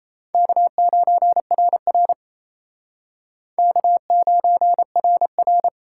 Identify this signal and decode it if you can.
Morse code, heard as a keyed tone, about 36 wpm, 700 Hz. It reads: K9RR K9RR